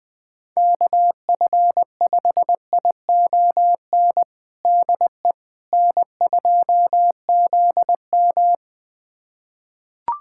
Morse code, heard as a keyed tone, about 20 wpm, 700 Hz. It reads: KF5ION DE N2ZM
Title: USA Callsign DE Callsign 20wpm